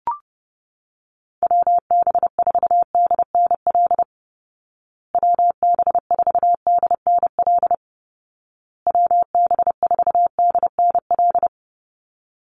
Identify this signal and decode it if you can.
Morse code, heard as a keyed tone, about 30 wpm, 700 Hz. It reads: WB4DNL WB4DNL WB4DNL
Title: Call Signs Repeated 2x - 30wpm